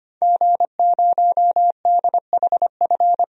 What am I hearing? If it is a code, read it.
Morse code, heard as a keyed tone, about 25 wpm, 700 Hz. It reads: G0DHF
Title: DXCC-100 Call Signs - 5 Characters - Repeated 2x - 25wpm